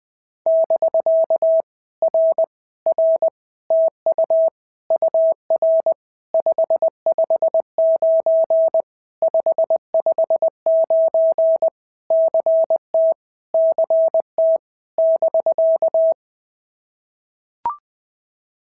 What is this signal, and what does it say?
Morse code, heard as a keyed tone, about 19 wpm, 650 Hz. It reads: <BK> R R TU UR 559 559 CT CT <BK>
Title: POTA QSO Practice - 20wpm